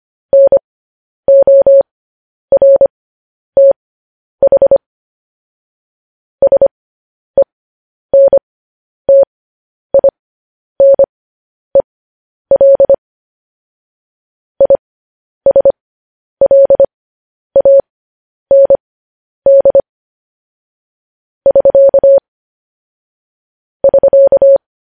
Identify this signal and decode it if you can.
Morse code, heard as a keyed tone, about 25 wpm, spaced as for 10 wpm, 550 Hz. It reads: NORTH SENTINEL ISLAND <SK> <SK>
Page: news headlines in morse code at 10 wpm